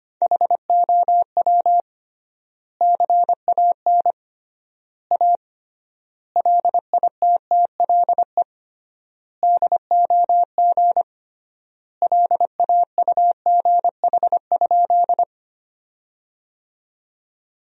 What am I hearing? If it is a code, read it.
Morse code, heard as a keyed tone, about 25 wpm, 700 Hz. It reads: HOW CAN A LITTLE DOG LAUGH?
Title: 3x Word-Spacing - Sentences from Top 500 Words - 25wpm